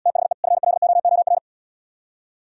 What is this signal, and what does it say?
Morse code, heard as a keyed tone, about 77 wpm, 700 Hz. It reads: THE FLOOR